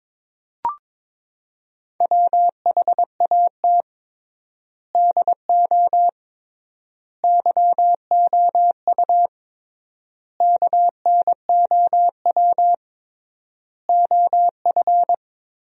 Morse code, heard as a keyed tone, about 22 wpm, 700 Hz. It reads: WHAT DO YOU KNOW OF
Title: 3x Word-Spacing - Sentences from Top 100 Words - 22wpm